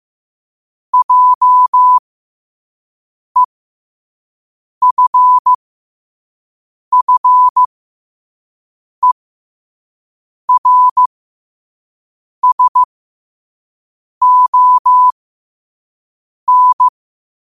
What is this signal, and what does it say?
Morse code, heard as a keyed tone, about 15 wpm, spaced as for 5 wpm, 1000 Hz. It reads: JEFFERSON
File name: qotdfiles / 5wpm